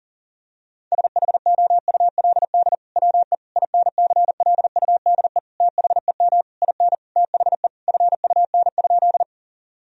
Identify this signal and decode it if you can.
Morse code, heard as a keyed tone, about 40 wpm, 700 Hz. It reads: SHOULD WE INCLUDE THEM IN THE FUN?
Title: Sentences from Top 1000 Words - Review Part 3 - 40wpm